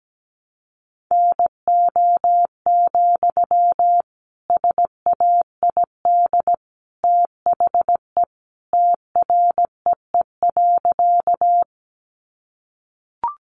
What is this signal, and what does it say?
Morse code, heard as a keyed tone, about 17 wpm, 700 Hz. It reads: NO, SAID THE TREE.
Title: Sentences from Top 300 Words 17wpm